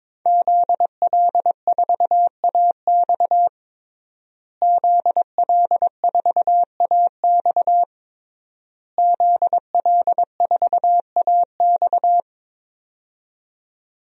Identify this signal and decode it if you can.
Morse code, heard as a keyed tone, about 22 wpm, 700 Hz. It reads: ZL4AX ZL4AX ZL4AX
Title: DXCC-100 Call Signs - 5 Characters - Repeated 3x - 22wpm